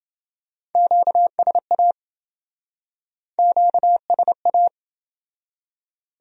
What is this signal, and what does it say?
Morse code, heard as a keyed tone, about 28 wpm, 700 Hz. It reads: QSA QSA